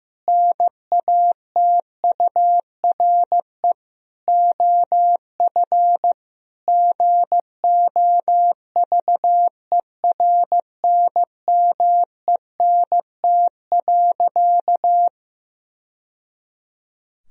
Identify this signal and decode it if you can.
Morse code, heard as a keyed tone, about 15 wpm, 700 Hz. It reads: NATURE OF GOVERNMENT.
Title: Sentences from Top 1000 Words - Review Part 3 - 15wpm